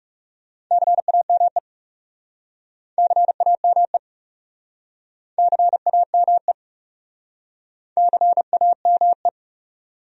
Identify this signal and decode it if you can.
Morse code, heard as a keyed tone, about 41 wpm, 700 Hz. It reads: CAME CAME CAME CAME